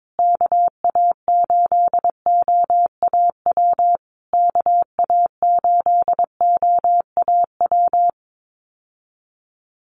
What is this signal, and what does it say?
Morse code, heard as a keyed tone, about 22 wpm, 700 Hz. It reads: KA8OAW KA8OAW
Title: USA Call Signs Repeated 3x 22wpm